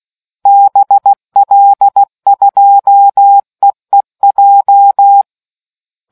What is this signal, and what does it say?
Morse code, heard as a keyed tone, about 16 wpm, 800 Hz. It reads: BL2EEJ